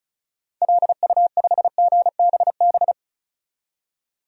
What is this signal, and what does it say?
Morse code, heard as a keyed tone, about 34 wpm, 700 Hz. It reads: LU5GBB
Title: DXCC-100 Call Signs - 6 Characters - 35wpm